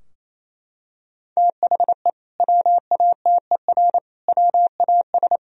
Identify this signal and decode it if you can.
Morse code, heard as a keyed tone, about 28 wpm, 700 Hz. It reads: THE WATER WAS
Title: Sentences from Top 1000 Words - Review Part 3 - 28wpm